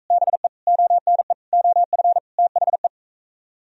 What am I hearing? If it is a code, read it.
Morse code, heard as a keyed tone, about 44 wpm, 700 Hz. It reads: BE ONE OF THE